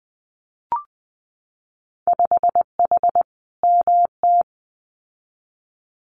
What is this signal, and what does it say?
Morse code, heard as a keyed tone, about 20 wpm, 700 Hz. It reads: 5H MT
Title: Winter Field Day Exchanges Encore 20wpm